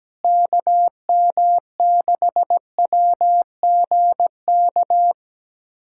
Morse code, heard as a keyed tone, about 17 wpm, 700 Hz. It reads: KM6WGK
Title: USA Callsign DE Callsign 17wpm